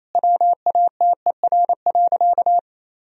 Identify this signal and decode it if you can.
Morse code, heard as a keyed tone, about 28 wpm, 700 Hz. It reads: WATER.
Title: Sentences from Top 500 Words and Abbreviations 28wpm